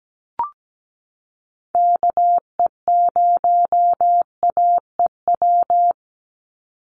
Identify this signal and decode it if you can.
Morse code, heard as a keyed tone, about 17 wpm, 700 Hz. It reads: KE0AEW
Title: USA Call Signs 17wpm